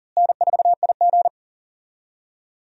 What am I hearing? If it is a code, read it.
Morse code, heard as a keyed tone, about 38 wpm, 700 Hz. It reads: N4IG
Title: USA Call Signs - 4 Characters - Repeated 2x - 40wpm